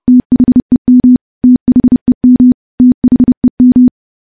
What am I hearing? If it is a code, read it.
Morse code, heard as a keyed tone, about 32 wpm, 250 Hz. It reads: THEM THEM THEM
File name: morse / morse-training-30WPM